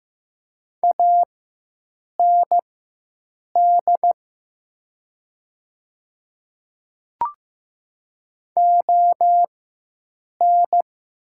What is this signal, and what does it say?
Morse code, heard as a keyed tone, about 15 wpm, spaced as for 7 wpm, 700 Hz. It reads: AND ON
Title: 4x Character Spacing - Top 100 Words - 15wpm